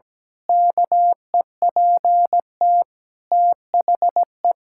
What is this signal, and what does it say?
Morse code, heard as a keyed tone, about 17 wpm, 700 Hz. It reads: KEPT THE
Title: Pangram Challenge v2 - 17wpm